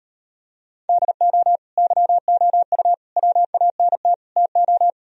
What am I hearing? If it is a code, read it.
Morse code, heard as a keyed tone, about 38 wpm, 700 Hz. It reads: DO YOU WANT TO